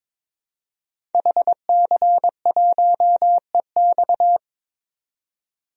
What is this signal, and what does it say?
Morse code, heard as a keyed tone, about 22 wpm, 700 Hz. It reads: HC1EX
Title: DXCC-100 Call Signs - 5 Characters - 22wpm